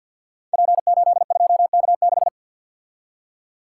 Morse code, heard as a keyed tone, about 52 wpm, 700 Hz. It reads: P82K6
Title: Code Groups - 5 Letter:Numbers - 50wpm